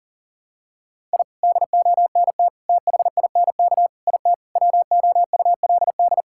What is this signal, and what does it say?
Morse code, heard as a keyed tone, about 40 wpm, 700 Hz. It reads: I DONT THINK IT WOULD
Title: Sentences from Top 1000 Words - Review Part 6 - 40wpm